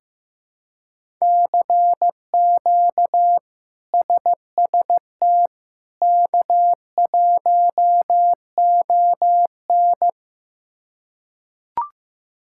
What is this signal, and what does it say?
Morse code, heard as a keyed tone, about 15 wpm, 700 Hz. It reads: CQ SST K1ON
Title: K1USN SST QSO Practice - 15wpm